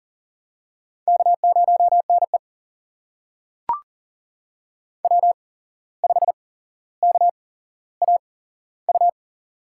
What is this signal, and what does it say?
Morse code, heard as a keyed tone, about 40 wpm, spaced as for 16 wpm, 700 Hz. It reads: K0NE W5KAU
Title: USA Call Signs - 8x Character Spacing - 40wpm